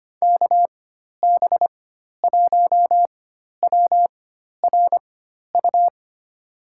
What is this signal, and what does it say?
Morse code, heard as a keyed tone, about 25 wpm, spaced as for 12 wpm, 700 Hz. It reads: KB1WRU
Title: USA Call Signs - 6 Characters - 4x Character Spacing - 25wpm